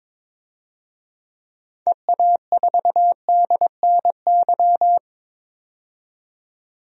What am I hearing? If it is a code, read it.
Morse code, heard as a keyed tone, about 22 wpm, 700 Hz. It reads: EA4DNY
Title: DXCC-100 Call Signs - 6 Characters - 22wpm